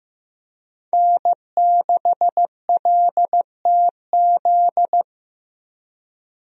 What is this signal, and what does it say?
Morse code, heard as a keyed tone, about 16 wpm, 700 Hz. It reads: N6LTZ
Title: USA Call Signs - 5 Characters - Repeated 3x - 15wpm